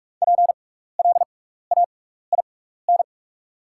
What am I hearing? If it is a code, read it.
Morse code, heard as a keyed tone, about 46 wpm, spaced as for 16 wpm, 700 Hz. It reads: PLAIN
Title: 6x Character Spacing - Top 500 Words - 45wpm